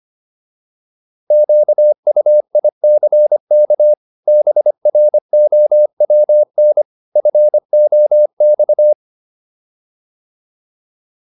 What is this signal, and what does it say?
Morse code, heard as a keyed tone, about 25 wpm, 600 Hz. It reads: QUICK BROWN FOX
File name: PR403 25